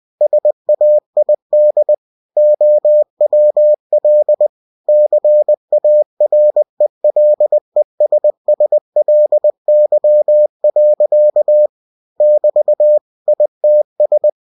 Morse code, heard as a keyed tone, about 20 wpm, 600 Hz. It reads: SAID OWL CARELESSLY. <BT> ITS